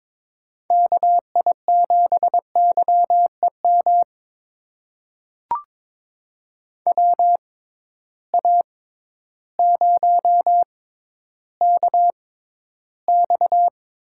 Morse code, heard as a keyed tone, about 22 wpm, spaced as for 12 wpm, 700 Hz. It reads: KI7YEM WA0KX
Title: USA Call Signs - 6 Characters - 6x Character Spacing - 22wpm